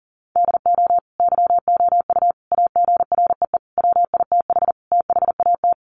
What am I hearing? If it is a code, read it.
Morse code, heard as a keyed tone, about 40 wpm, 700 Hz. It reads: DO YOU AGREE WITH THAT
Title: Sets of 5 Words Encore 40wpm